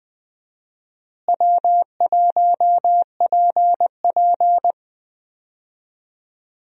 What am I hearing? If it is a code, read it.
Morse code, heard as a keyed tone, about 20 wpm, 700 Hz. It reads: W1PP
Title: USA Call Signs - 4 Characters - Repeated 2x - 20wpm